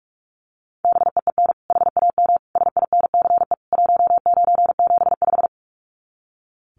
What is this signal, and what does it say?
Morse code, heard as a keyed tone, about 44 wpm, 700 Hz. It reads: BEEN HAM SINCE 1975
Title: Common QSO Phrases 45wpm